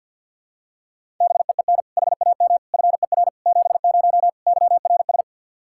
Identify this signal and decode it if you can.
Morse code, heard as a keyed tone, about 51 wpm, 700 Hz. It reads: BEEN HAM FER 70 YRS
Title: Practice QSO - Exchange 3 - More Info - 50wpm